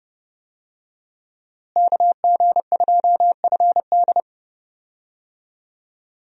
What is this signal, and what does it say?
Morse code, heard as a keyed tone, about 32 wpm, 700 Hz. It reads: KG2FD
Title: USA Call Signs - 5 Characters - Repeated 2x - 30wpm